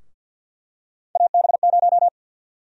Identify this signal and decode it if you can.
Morse code, heard as a keyed tone, about 52 wpm, 700 Hz. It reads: AB0
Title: USA Call Sign Prefixes - 50wpm